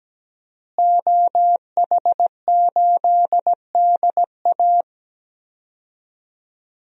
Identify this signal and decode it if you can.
Morse code, heard as a keyed tone, about 17 wpm, 700 Hz. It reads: OH8DA
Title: DXCC-100 Call Signs - 5 Characters - Repeated 2x - 17wpm